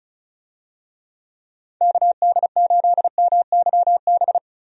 Morse code, heard as a keyed tone, about 36 wpm, 700 Hz. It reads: KD8MYB
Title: USA Call Signs - 6 Characters - 35wpm